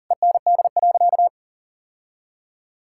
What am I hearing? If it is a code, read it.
Morse code, heard as a keyed tone, about 40 wpm, 700 Hz. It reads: END.
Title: Sentences from Top 1600 Words 40wpm